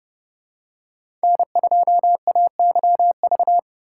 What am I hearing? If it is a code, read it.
Morse code, heard as a keyed tone, about 29 wpm, 700 Hz. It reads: N2AYV
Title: USA Call Signs - 5 Characters - 4x Character Spacing - 30wpm